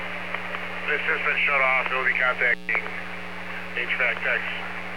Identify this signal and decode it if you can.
Emergency Alert System (SAME) tones, heard as harsh buzzing bursts, with speech. It is heard from 0.8 to 4.4 s.